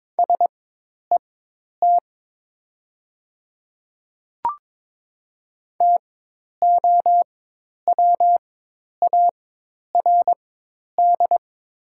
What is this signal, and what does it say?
Morse code, heard as a keyed tone, about 22 wpm, spaced as for 10 wpm, 700 Hz. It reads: SET TOWARD
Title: 4x Character Spacing - Top 500 Words - 22wpm